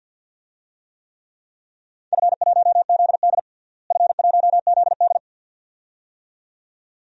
Morse code, heard as a keyed tone, about 52 wpm, 700 Hz. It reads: F1ZD F1ZD
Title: DXCC-100 Call Signs - 4 Characters - Repeated 2x - 50wpm